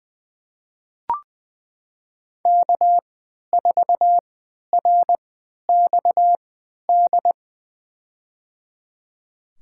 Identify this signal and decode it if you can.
Morse code, heard as a keyed tone, about 20 wpm, spaced as for 11 wpm, 700 Hz. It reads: K4RXD
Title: USA Call Signs - 3x Character Spacing - 20wpm